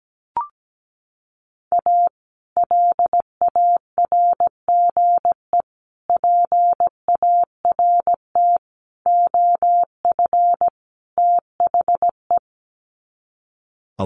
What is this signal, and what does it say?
Morse code, heard as a keyed tone, about 17 wpm, 700 Hz. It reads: A LARGE PART OF THE
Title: Sets of 5 Words Encore2 17wpm